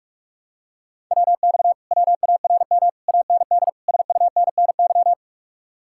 Morse code, heard as a keyed tone, about 44 wpm, 700 Hz. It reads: WX WARM AND SUNNY